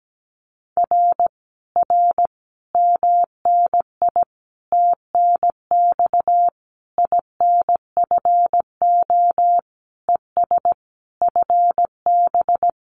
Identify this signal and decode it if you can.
Morse code, heard as a keyed tone, about 17 wpm, 700 Hz. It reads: R R MNI TNX INFO ES FB
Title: Common QSO Phrases Encore 17wpm